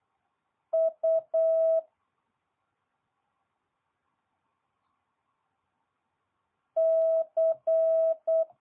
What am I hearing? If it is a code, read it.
Morse code, heard as a keyed tone, about 8 wpm, 650 Hz. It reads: U C
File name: jh8cbh / 2022cw